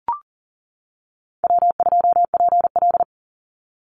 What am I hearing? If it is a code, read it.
Morse code, heard as a keyed tone, about 40 wpm, 700 Hz. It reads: W2PL
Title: Easy Call Signs - 40wpm